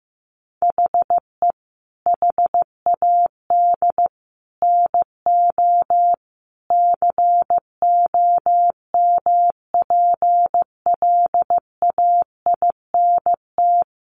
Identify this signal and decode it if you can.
Morse code, heard as a keyed tone, about 15 wpm, 700 Hz. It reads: HE HAD NO COMPLAINT
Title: Sentences from Top 1400 Words 15wpm